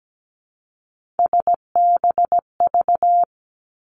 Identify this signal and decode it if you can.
Morse code, heard as a keyed tone, about 17 wpm, 700 Hz. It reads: SBV
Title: Three Challenging Letter-Number Combinations 17wpm